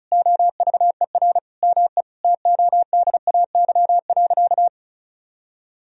Morse code, heard as a keyed tone, about 36 wpm, 700 Hz. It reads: OVER ME TODAY.
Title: Sentences from Top 1000 Words - Review Part 4 - 35wpm